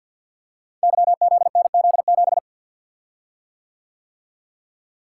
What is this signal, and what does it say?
Morse code, heard as a keyed tone, about 50 wpm, 690 Hz. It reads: YZNZ7
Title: Code Groups - 5 Letter:Numbers - 50wpm